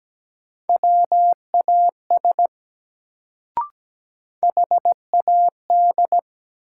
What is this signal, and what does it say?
Morse code, heard as a keyed tone, about 17 wpm, 700 Hz. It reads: WAS HAD
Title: Sentences from Top 100 Words - New Word Drill - 17wpm